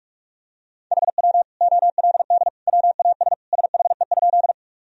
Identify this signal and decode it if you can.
Morse code, heard as a keyed tone, about 46 wpm, 700 Hz. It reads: HW OLD WAS SHE?
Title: Sentences from Top 200 Words and Abbreviations 45wpm